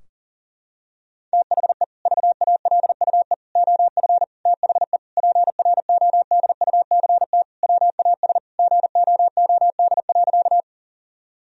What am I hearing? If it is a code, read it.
Morse code, heard as a keyed tone, about 40 wpm, 700 Hz. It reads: THE VALUE OF THE PRODUCT WAS GOOD.
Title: Sentences from Top 1000 Words - Review Part 3 - 40wpm